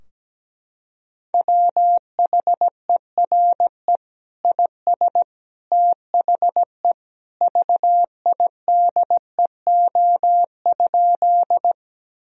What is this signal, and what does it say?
Morse code, heard as a keyed tone, about 17 wpm, 700 Hz. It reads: WHERE IS THE VIDEO?
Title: Sentences from Top 1000 Words - Review Part 4 - 17wpm